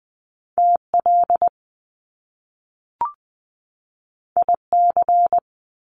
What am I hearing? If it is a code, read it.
Morse code, heard as a keyed tone, about 20 wpm, 700 Hz. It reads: TL IC
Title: Top Two Letter Combinations in Words 20wpm